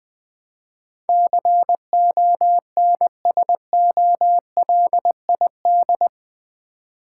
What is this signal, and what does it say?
Morse code, heard as a keyed tone, about 21 wpm, 700 Hz. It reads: CONSOLID
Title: Build A Word - 4 Letters At A Time - 11-Letter Words - 20wpm